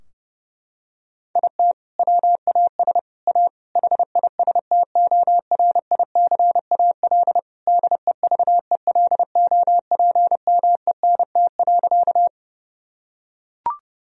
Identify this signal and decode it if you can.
Morse code, heard as a keyed tone, about 30 wpm, 700 Hz. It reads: IT WAS A HISTORICAL DEVELOPMENT.
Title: Sentences from Top 1000 Words - Review Part 4 - 30wpm